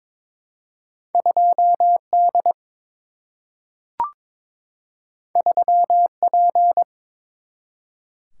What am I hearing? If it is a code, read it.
Morse code, heard as a keyed tone, about 22 wpm, 700 Hz. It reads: 2 D 3 P
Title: Single-Letter Single-Number 22wpm